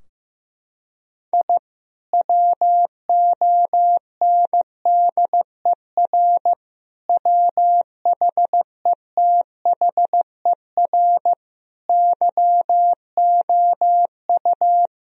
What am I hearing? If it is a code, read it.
Morse code, heard as a keyed tone, about 15 wpm, 700 Hz. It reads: I WONDER WHETHER YOU
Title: Sentences from Top 1000 Words - Review Part 3 - 15wpm